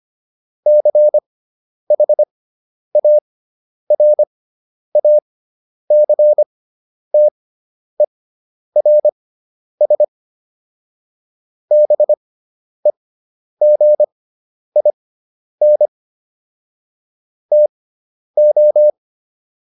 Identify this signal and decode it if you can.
Morse code, heard as a keyed tone, about 25 wpm, spaced as for 10 wpm, 600 Hz. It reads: CHARACTERS BEGIN TO